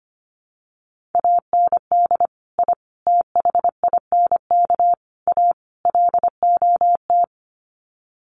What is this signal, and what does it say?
Morse code, heard as a keyed tone, about 25 wpm, 700 Hz. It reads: AND I THINK A LOT
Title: Sets of 5 Words Encore2 25wpm